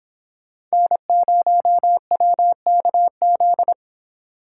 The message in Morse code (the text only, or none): N0WKZ